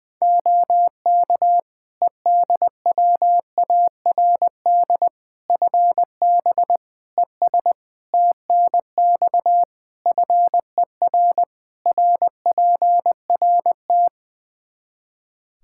Morse code, heard as a keyed tone, about 20 wpm, 700 Hz. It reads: OK EDWARD FB ES TNX FER RPRT
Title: Practice QSO - Exchange 2 - Info - 20wpm